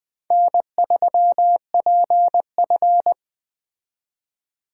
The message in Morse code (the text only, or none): N3PF